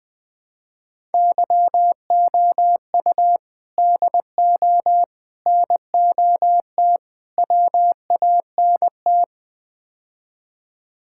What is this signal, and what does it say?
Morse code, heard as a keyed tone, about 20 wpm, 700 Hz. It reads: YOU DO NOT WANT
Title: Build A Phrase - 5-Word Phrases - 20wpm